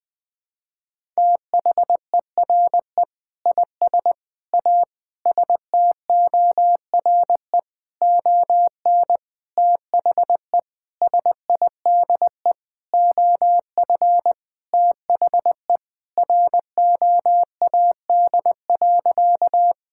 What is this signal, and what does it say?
Morse code, heard as a keyed tone, about 20 wpm, 700 Hz. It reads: THERE IS A STORE ON THE SIDE OF THE ROAD.
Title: Sentences with 6-Word Phrases - 20wpm